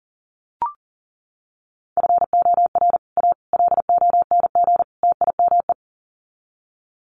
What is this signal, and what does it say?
Morse code, heard as a keyed tone, about 40 wpm, 700 Hz. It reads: FOR A LONG TIME
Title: Sets of 4 Words 40wpm